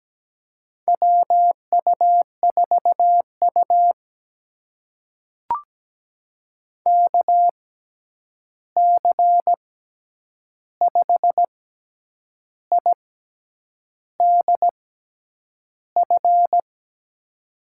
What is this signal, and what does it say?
Morse code, heard as a keyed tone, about 17 wpm, spaced as for 8 wpm, 700 Hz. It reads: WU4U KC5IDF
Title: USA Call Signs - 6x Character Spacing - 17wpm